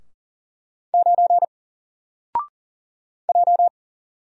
Morse code, heard as a keyed tone, about 40 wpm, 700 Hz. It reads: 9 J
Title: Single Letter-Number Alt 40wpm